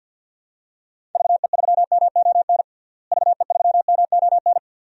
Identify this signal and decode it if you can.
Morse code, heard as a keyed tone, about 52 wpm, 700 Hz. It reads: VE3MON VE3MON
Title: DXCC-100 Call Signs - 6 Characters - Repeated 3x - 50wpm